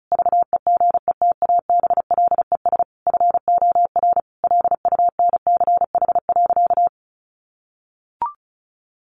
Morse code, heard as a keyed tone, about 34 wpm, 700 Hz. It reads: VEGETABLES FOR LUNCH.
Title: Easy Phrases 35wpm